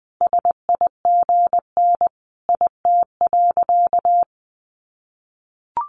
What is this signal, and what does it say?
Morse code, heard as a keyed tone, about 20 wpm, 700 Hz. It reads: SIGN IT.
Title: Sentences from Top 1000 Words 20wpm